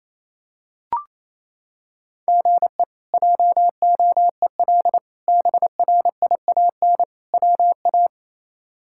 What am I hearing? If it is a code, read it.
Morse code, heard as a keyed tone, about 28 wpm, 700 Hz. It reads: GE JOEL BRIAN WA
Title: K1USN SST QSO Practice - 28wpm